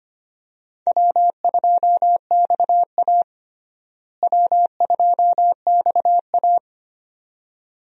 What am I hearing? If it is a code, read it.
Morse code, heard as a keyed tone, about 25 wpm, 700 Hz. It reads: W2XA W2XA
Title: USA Call Signs - 4 Characters - Repeated 2x - 25wpm